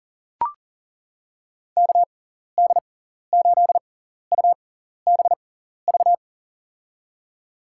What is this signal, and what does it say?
Morse code, heard as a keyed tone, about 40 wpm, spaced as for 14 wpm, 700 Hz. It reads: KD8UBV
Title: USA Call Signs - 6 Characters - 6x Character Spacing - 40wpm